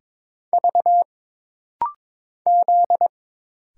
Morse code, heard as a keyed tone, about 22 wpm, 700 Hz. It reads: V Z
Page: This rapid-fire timing is geared to push your ICR (Instant Character Recognition) skills.